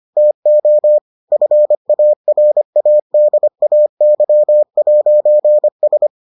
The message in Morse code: TO FARADAY'S

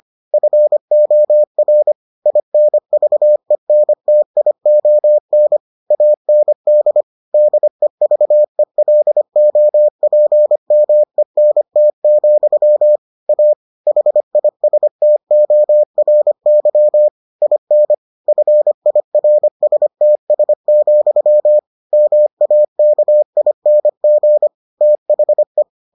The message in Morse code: FOR INVENTION AND DEVELOPMENT, A HISTORY IN FIRSTS, MAKING THE

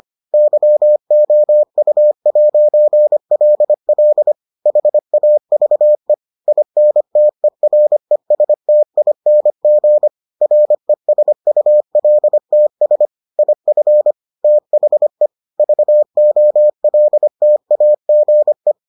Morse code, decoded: YOU'LL HAVE INTERESTING RESULTS IF THE VOLTAGE

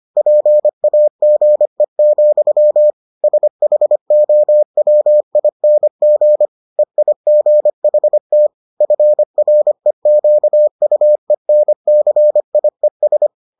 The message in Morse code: PAGE, SHOWING EIGHT FREQUENCIES

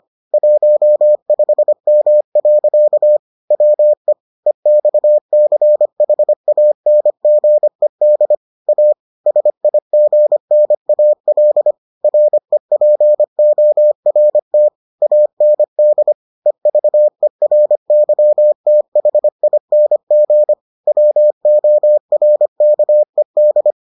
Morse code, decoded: 15M. WE EXCHANGED A SIGNAL REPORT AND EVERYTHING WORKED

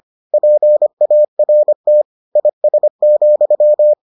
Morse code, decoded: PART IS,

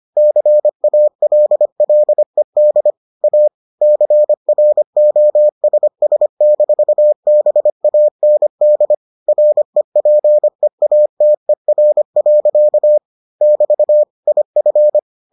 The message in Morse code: CALLED A CROSS-BAND REPEATER. <BT> IF

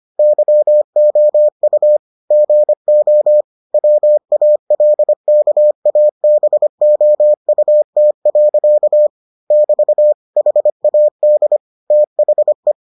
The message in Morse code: YOU GO WALKABOUT. <BT> HAD THE

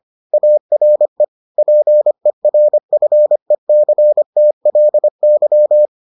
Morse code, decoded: ARE PERFECTLY

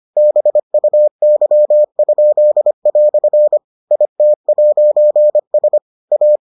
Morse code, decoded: BUY?" IT'S A

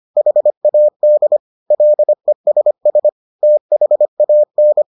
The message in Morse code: HAD LESS THAN

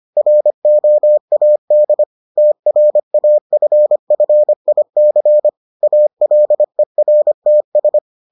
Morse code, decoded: ROAD TRAFFIC ALERTS

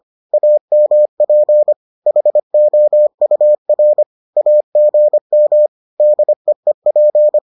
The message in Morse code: AMP HOUR AGM DEEP

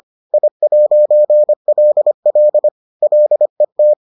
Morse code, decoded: I'LL LET

